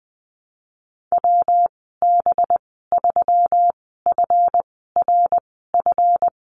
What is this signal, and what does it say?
Morse code, heard as a keyed tone, about 20 wpm, spaced as for 15 wpm, 700 Hz. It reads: WB3FRF
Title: 200 Lesson 32g - Letter k - Call Signs 20wpm